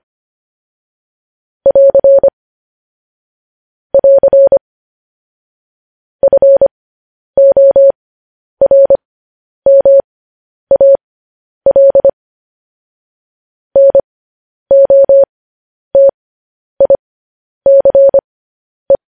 Morse code, heard as a keyed tone, about 25 wpm, spaced as for 10 wpm, 550 Hz. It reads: <AR> <AR> FORMAL NOTICE
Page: news headlines in morse code at 10 wpm